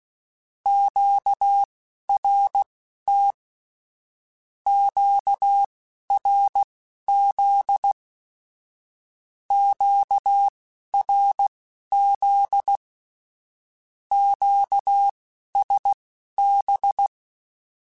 Morse code, heard as a keyed tone, about 16 wpm, spaced as for 11 wpm, 800 Hz. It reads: QRT QRZ QRZ QSB
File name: (spaced) / Q-Calls Spaced-mp3s / Q Calls sp Once @15